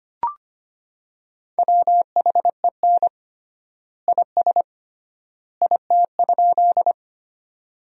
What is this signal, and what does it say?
Morse code, heard as a keyed tone, about 25 wpm, 700 Hz. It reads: WHEN IS IT?
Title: 3x Word-Spacing - Sentences from Top 100 Words - 25wpm